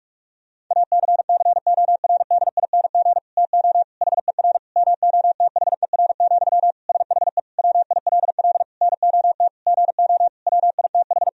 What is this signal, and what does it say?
Morse code, heard as a keyed tone, about 46 wpm, 700 Hz. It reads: ACCORDING TO HER MOTHER, SHE WILL NOT GO WITH
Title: Sentences from Top 1000 Words - Review Part 4 - 45wpm